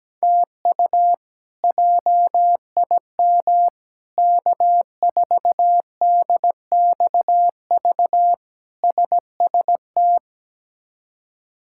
Morse code, heard as a keyed tone, about 17 wpm, 700 Hz. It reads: TU JIM K4DXV SST
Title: K1USN SST QSO Practice - 17wpm